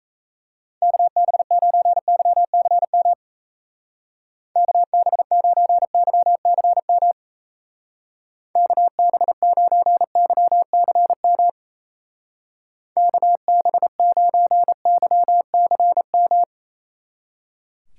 Morse code, heard as a keyed tone, about 36 wpm, 700 Hz. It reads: KB9YCM KB9YCM KB9YCM KB9YCM